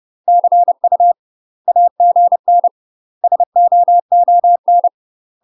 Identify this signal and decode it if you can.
Morse code, heard as a keyed tone, about 30 wpm, 700 Hz. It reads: CU AGN SOON
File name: UK MASTER QSOs TRAINING/30WPM